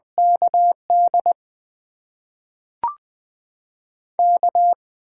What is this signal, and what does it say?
Morse code, heard as a keyed tone, about 20 wpm, spaced as for 5 wpm, 700 Hz. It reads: KDK
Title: USA Call Signs - 6 Characters - 8x Character Spacing - 20wpm